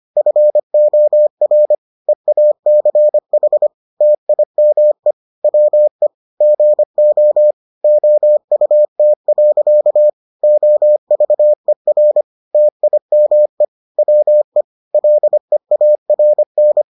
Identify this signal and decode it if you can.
Morse code, heard as a keyed tone, about 25 wpm, 600 Hz. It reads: FOR EACH TIME WE GO OUT. OVER TIME WE LEARN